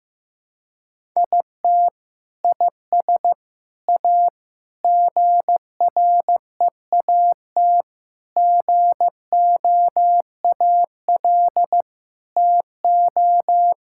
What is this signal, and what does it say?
Morse code, heard as a keyed tone, about 16 wpm, 700 Hz. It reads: IT IS A GREAT GOAL TO
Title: Sentences from Top 1000 Words - Review Part 5 - 15wpm